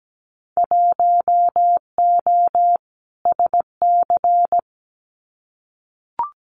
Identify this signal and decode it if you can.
Morse code, heard as a keyed tone, about 17 wpm, 700 Hz. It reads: 1O SC
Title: Winter Field Day Exchanges 17wpm